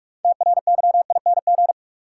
Morse code, heard as a keyed tone, about 42 wpm, 700 Hz. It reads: TRYING
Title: Sets of 2 Words - New Word Drill - 45wpm